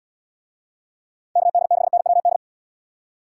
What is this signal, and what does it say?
Morse code, heard as a keyed tone, about 70 wpm, 700 Hz. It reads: KD6NWD